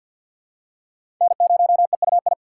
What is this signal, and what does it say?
Morse code, heard as a keyed tone, about 52 wpm, 690 Hz. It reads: N0EUI
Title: USA Call Signs - 5 Characters - 2x Character Spacing - 50wpm